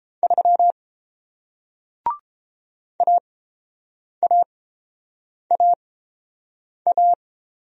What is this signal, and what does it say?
Morse code, heard as a keyed tone, about 31 wpm, spaced as for 7 wpm, 700 Hz. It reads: 3 AAAA